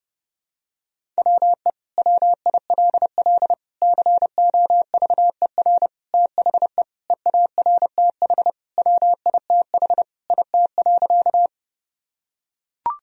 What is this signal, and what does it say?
Morse code, heard as a keyed tone, about 30 wpm, 700 Hz. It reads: WE WILL COVER THE EARTH WITH IT.
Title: Sentences from Top 1000 Words - Review Part 1 - 30wpm